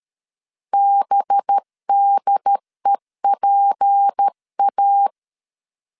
Morse code, heard as a keyed tone, about 13 wpm, 800 Hz. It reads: BDEPA
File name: CW / DET / 1 6 1500C S C 12wpm 3533